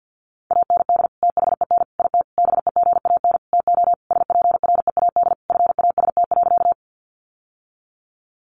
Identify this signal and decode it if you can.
Morse code, heard as a keyed tone, about 50 wpm, 700 Hz. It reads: AND THEN IT BEGAN TO SPREAD FAST.
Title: Sentences from Top 1200 Words 50wpm